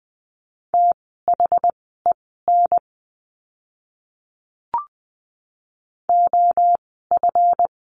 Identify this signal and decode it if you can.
Morse code, heard as a keyed tone, about 20 wpm, spaced as for 14 wpm, 700 Hz. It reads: THEN OF
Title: 237 Lesson 38f - Letter q - Words 20wpm